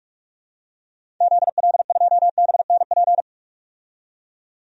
Morse code, heard as a keyed tone, about 46 wpm, 700 Hz. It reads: ZL2BNP
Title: DXCC-100 Call Signs - 6 Characters - 45wpm